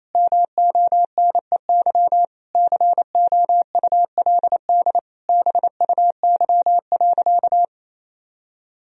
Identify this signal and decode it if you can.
Morse code, heard as a keyed tone, about 28 wpm, 700 Hz. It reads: MONEY COULD BUY.
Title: Sentences from Top 1000 Words - Review Part 4 - 28wpm